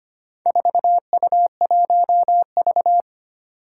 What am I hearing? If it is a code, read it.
Morse code, heard as a keyed tone, about 25 wpm, 700 Hz. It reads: 4U1V
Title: Four Challenging Letter-Number Combinations 25wpm